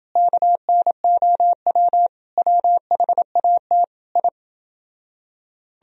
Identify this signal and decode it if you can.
Morse code, heard as a keyed tone, about 30 wpm, 700 Hz. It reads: KNOW WHAT I